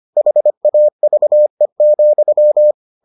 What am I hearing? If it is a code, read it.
Morse code, heard as a keyed tone, about 25 wpm, 600 Hz. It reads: HAVE,